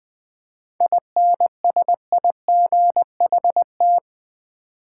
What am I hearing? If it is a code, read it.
Morse code, heard as a keyed tone, about 20 wpm, 700 Hz. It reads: INSIGHT
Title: 7 Letter Words - First 3 Letters Given - 20wpm